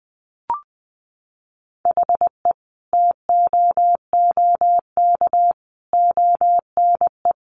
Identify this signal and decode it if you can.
Morse code, heard as a keyed tone, about 20 wpm, 700 Hz. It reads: HE TOOK ONE
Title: Sentences from Top 600 Words 20wpm